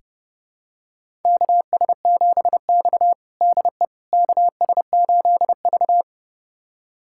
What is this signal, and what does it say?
Morse code, heard as a keyed tone, about 30 wpm, 700 Hz. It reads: KS7X DE KS8V
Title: USA Callsign DE Callsign - 4 Characters - 30wpm